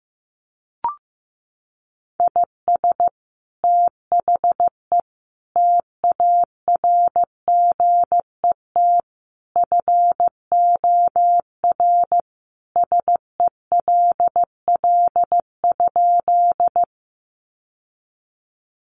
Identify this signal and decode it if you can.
Morse code, heard as a keyed tone, about 15 wpm, 700 Hz. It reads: IS THE TARGET FOR SELL?
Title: Sentences from Top 1200 Words 15wpm